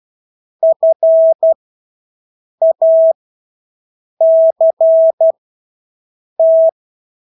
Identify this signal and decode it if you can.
Morse code, heard as a keyed tone, about 12 wpm, spaced as for 6 wpm, 650 Hz. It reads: FACT
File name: Engelske ord 12-6 wpm